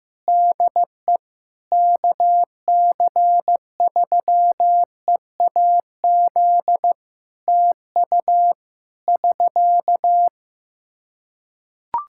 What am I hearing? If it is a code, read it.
Morse code, heard as a keyed tone, about 15 wpm, 700 Hz. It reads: DE KC3EAZ TU <SK>
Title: Practice QSO - Exchange 4 - Ending - 15wpm